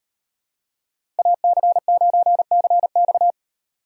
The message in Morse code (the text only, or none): AC9CX